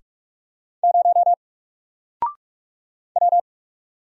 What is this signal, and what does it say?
Morse code, heard as a keyed tone, about 44 wpm, 700 Hz. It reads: 0 W
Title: Single Letter-Number Alt 45wpm